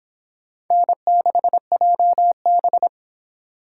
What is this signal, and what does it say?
Morse code, heard as a keyed tone, about 26 wpm, 700 Hz. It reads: N6JB